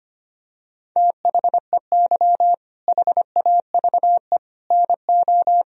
Morse code, heard as a keyed tone, about 25 wpm, 700 Hz. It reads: THEY HAVE NO
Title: Sentences from Top 1000 Words - Review Part 1 - 25wpm